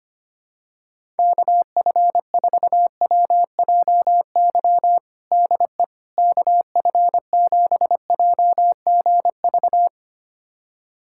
Morse code, heard as a keyed tone, about 25 wpm, 700 Hz. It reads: KF4WJY DE KF7JGV
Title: USA Callsign DE Callsign 25wpm